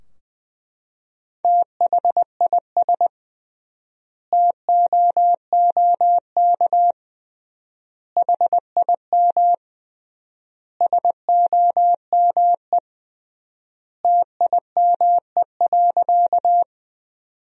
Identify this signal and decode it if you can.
Morse code, heard as a keyed tone, about 20 wpm, 700 Hz. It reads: THIS TOOK HIM SOME TIME.
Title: 3x Word-Spacing - Sentences from Top 300 Words - 20wpm